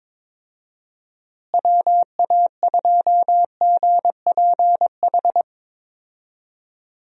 Morse code, heard as a keyed tone, about 22 wpm, 700 Hz. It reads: WA2GPH
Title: USA Call Signs - 6 Characters - Repeated 3x - 22wpm